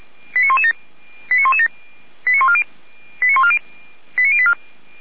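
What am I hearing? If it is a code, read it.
Five-tone selective call (ZVEI), heard as quick runs of five tones, tones of about 70 ms. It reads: 89117 89117 89140 89084
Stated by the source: ZVEI five-tone selective call received on an SDR.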